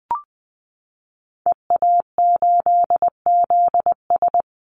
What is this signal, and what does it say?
Morse code, heard as a keyed tone, about 20 wpm, 700 Hz. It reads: EA8ZS
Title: Call Signs - 20wpm